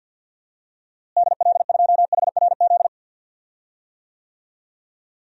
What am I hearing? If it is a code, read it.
Morse code, heard as a keyed tone, about 47 wpm, 700 Hz. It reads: DL2HRZ
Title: DXCC-100 Call Signs - 6 Characters - Repeated 3x - 50wpm